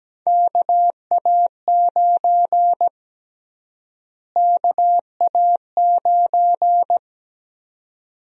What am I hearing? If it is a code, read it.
Morse code, heard as a keyed tone, about 17 wpm, 700 Hz. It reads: KA9 KA9
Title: USA Call Sign Prefixes - Repeated 2x - 17wpm